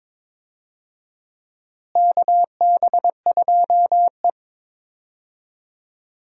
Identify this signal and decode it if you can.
Morse code, heard as a keyed tone, about 22 wpm, 700 Hz. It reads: KB2E
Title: USA Call Signs - 4 Characters - Repeated 3x - 22wpm